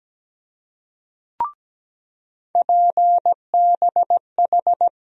Morse code, heard as a keyed tone, about 17 wpm, 700 Hz. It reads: PBH
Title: USA Call Sign Postfixes - 3 Letters 17wpm